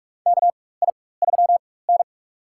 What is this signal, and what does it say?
Morse code, heard as a keyed tone, about 46 wpm, spaced as for 21 wpm, 700 Hz. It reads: KI3N
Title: USA Call Signs - 4x Character Spacing - 45wpm